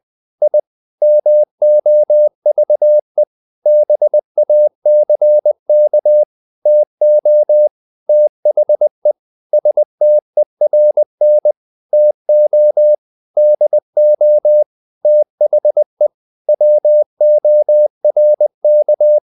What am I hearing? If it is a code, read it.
Morse code, heard as a keyed tone, about 20 wpm, 600 Hz. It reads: I MOVE BACK TO THE STERN TO DO THE WORK
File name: the old man and the sea part 2 20wpm